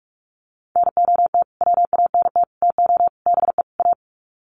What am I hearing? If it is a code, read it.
Morse code, heard as a keyed tone, about 44 wpm, 700 Hz. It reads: NOT WANT TO BE A
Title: Sets of 5 Words Encore 45wpm